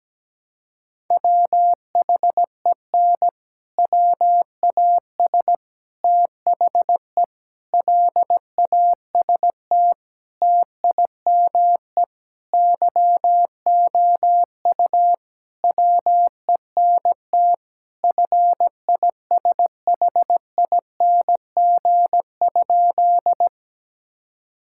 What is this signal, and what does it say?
Morse code, heard as a keyed tone, about 17 wpm, 700 Hz. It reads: WHEN WAS THE LAST TIME YOU WENT FISHING?
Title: Sentences from Top 1000 Words - Review Part 5 - 17wpm